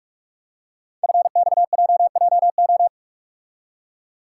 Morse code, heard as a keyed tone, about 44 wpm, 700 Hz. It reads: FXJJO
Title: Code Groups - 5 Letters - 45wpm